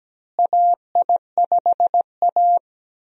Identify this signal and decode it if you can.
Morse code, heard as a keyed tone, about 17 wpm, 700 Hz. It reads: AI5A
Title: USA Call Signs - 4 Characters - Repeated 2x - 17wpm